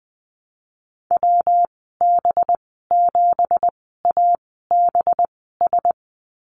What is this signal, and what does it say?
Morse code, heard as a keyed tone, about 20 wpm, spaced as for 15 wpm, 700 Hz. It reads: WB7ABS
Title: 207 Lesson 33g - Letter j - Call Signs 20wpm